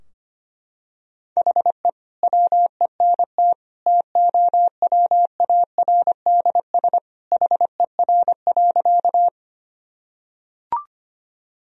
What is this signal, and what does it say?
Morse code, heard as a keyed tone, about 25 wpm, 700 Hz. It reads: HE WENT TOWARDS HER.
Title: Sentences from Top 1000 Words - Review Part 2 - 25wpm